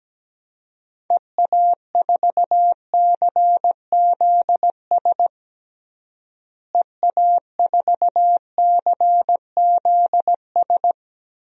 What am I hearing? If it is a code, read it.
Morse code, heard as a keyed tone, about 17 wpm, 700 Hz. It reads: EA4CZS EA4CZS
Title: DXCC-100 Call Signs - 6 Characters - Repeated 3x - 17wpm